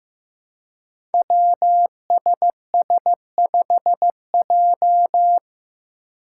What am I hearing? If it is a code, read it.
Morse code, heard as a keyed tone, about 15 wpm, 700 Hz. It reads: WSS5J
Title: Five Challenging Letter-Number Combinations 15wpm